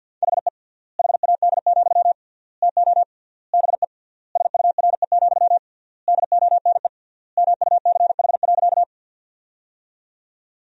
Morse code, heard as a keyed tone, about 48 wpm, 700 Hz. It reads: HE HAD, TO BE SURE, DONE MUCH.
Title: 3x Word-Spacing - Sentences from Top 500 Words - 50wpm